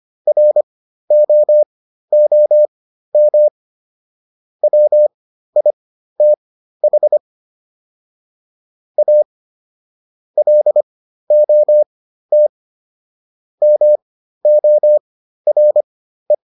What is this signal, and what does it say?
Morse code, heard as a keyed tone, about 25 wpm, spaced as for 13 wpm, 600 Hz. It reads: ROOM WITH A LOT MORE